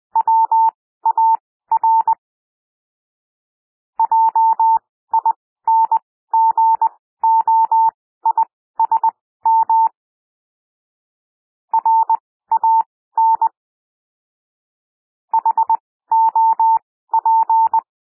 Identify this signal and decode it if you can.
Morse code, heard as a keyed tone, about 20 wpm, spaced as for 14 wpm, 900 Hz. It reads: WAR JINGOISM RAN HOP
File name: SOUND WORDS 20 wpm